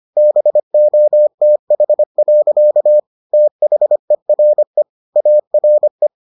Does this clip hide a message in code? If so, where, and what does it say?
Yes: Morse code from 0.2 to 6.1 s, heard as a keyed tone, about 25 wpm, 600 Hz. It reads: BOTH. THERE ARE